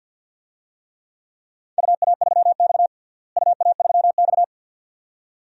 Morse code, heard as a keyed tone, about 49 wpm, 700 Hz. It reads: UA3X UA3X
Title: DXCC-100 Call Signs - 4 Characters - Repeated 2x - 50wpm